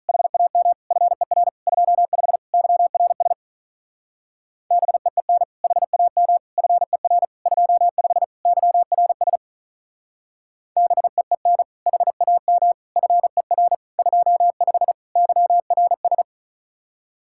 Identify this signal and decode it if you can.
Morse code, heard as a keyed tone, about 46 wpm, 700 Hz. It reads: HAM FER 25 YRS BEEN HAM FER 25 YRS BEEN HAM FER 25 YRS